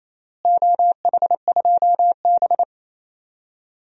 Morse code, heard as a keyed tone, about 28 wpm, 700 Hz. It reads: OH2B
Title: DXCC-100 Call Signs - 4 Characters - Repeated 3x - 28wpm